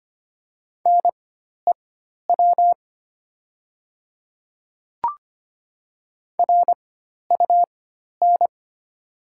Morse code, heard as a keyed tone, about 25 wpm, spaced as for 12 wpm, 700 Hz. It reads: NEW RUN
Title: 4x Character Spacing - Top 300 Words - 25wpm